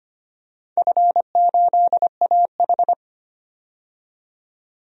Morse code, heard as a keyed tone, about 25 wpm, 700 Hz. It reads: F8AH
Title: DXCC-100 Call Signs - 4 Characters - 25wpm